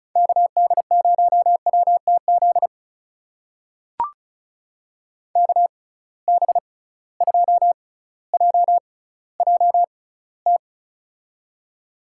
Morse code, heard as a keyed tone, about 36 wpm, spaced as for 18 wpm, 700 Hz. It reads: KD0WTZ KB2JJT
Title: USA Call Signs - 6 Characters - 6x Character Spacing - 35wpm